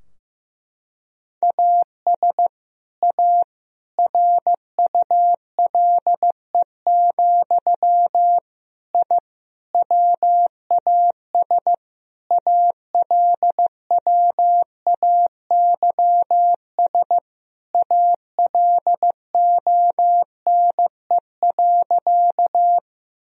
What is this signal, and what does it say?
Morse code, heard as a keyed tone, about 15 wpm, 700 Hz. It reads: AS A RULE, I WAS ALWAYS ALONE.
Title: Sentences from Top 1000 Words - Review Part 2 - 15wpm